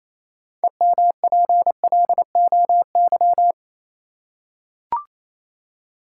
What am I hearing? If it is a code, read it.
Morse code, heard as a keyed tone, about 27 wpm, 700 Hz. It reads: EMPLOY
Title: Six Letter Words - 28wpm